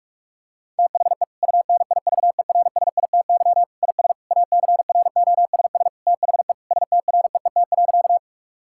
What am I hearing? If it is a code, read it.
Morse code, heard as a keyed tone, about 46 wpm, 700 Hz. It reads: THE UNIVERSITY IS ACROSS THE STREET.
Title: Sentences from Top 1000 Words - Review Part 5 - 45wpm